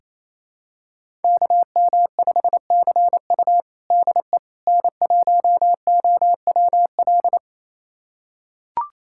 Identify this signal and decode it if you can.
Morse code, heard as a keyed tone, about 27 wpm, 700 Hz. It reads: KM5CU DE N1OWL
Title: USA Callsign DE Callsign - 5 Characters - 28wpm